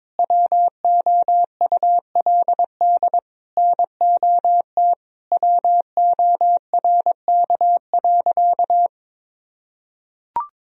Morse code, heard as a keyed tone, about 22 wpm, 700 Hz. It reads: WOULD NOT WORK.
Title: Sentences from Top 1000 Words - Review Part 3 - 22wpm